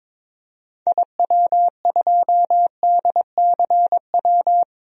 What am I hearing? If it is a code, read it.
Morse code, heard as a keyed tone, about 21 wpm, 700 Hz. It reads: IW2DCW
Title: DXCC-100 Call Signs - 6 Characters - 22wpm